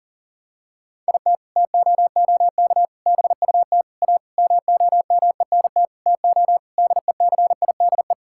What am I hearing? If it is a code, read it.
Morse code, heard as a keyed tone, about 40 wpm, 700 Hz. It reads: IT TOOK BUT A MOMENT TO DECIDE
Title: Sentences from Top 1600 Words 40wpm